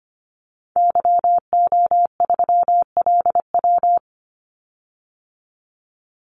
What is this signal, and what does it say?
Morse code, heard as a keyed tone, about 25 wpm, 700 Hz. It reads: YO3LW
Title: Call Signs Repeated 2x - 25wpm